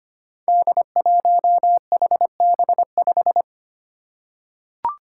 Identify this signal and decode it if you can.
Morse code, heard as a keyed tone, about 25 wpm, 700 Hz. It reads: D1HB5
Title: Five Challenging Letter-Number Combinations 25wpm